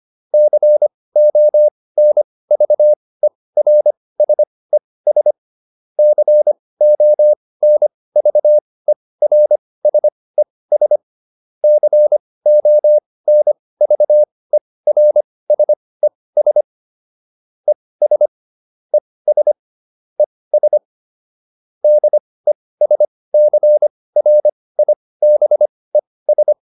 Morse code, heard as a keyed tone, about 25 wpm, spaced as for 18 wpm, 600 Hz. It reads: CONVERSES CONVERSES CONVERSES ES ES ES DESCRIBES